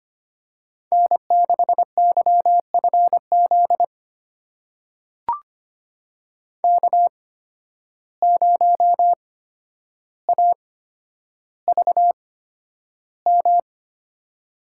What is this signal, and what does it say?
Morse code, heard as a keyed tone, about 25 wpm, spaced as for 11 wpm, 700 Hz. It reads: N6YFZ K0AVM
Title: USA Call Signs - 5 Characters - 8x Character Spacing - 25wpm